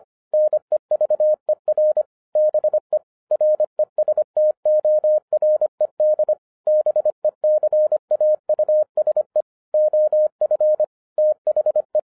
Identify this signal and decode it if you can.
Morse code, heard as a keyed tone, about 25 wpm, 610 Hz. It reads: NEVER BE RESTORED BECAUSE OF THE
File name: 2011-07-mp3 / n39 THE 3.11 2 25wpm